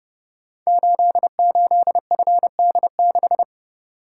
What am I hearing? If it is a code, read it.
Morse code, heard as a keyed tone, about 30 wpm, 700 Hz. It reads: 88FD6
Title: Code Groups - 5 Letter:Numbers - 30wpm